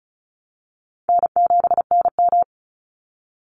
Morse code, heard as a keyed tone, about 34 wpm, 700 Hz. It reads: N7NM
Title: Easy Call Signs Repeated 3x - 35wpm